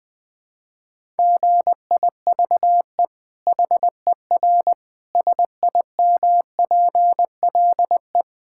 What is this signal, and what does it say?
Morse code, heard as a keyed tone, about 20 wpm, 700 Hz. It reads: GIVE HER SIMPLE
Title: Sentences from Top 1000 Words - Review Part 5 - 20wpm